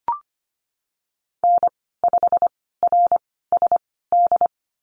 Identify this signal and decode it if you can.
Morse code, heard as a keyed tone, about 25 wpm, spaced as for 16 wpm, 700 Hz. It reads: N5RSD
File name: Lesson 15g - Number 5 - Call Signs 25wpm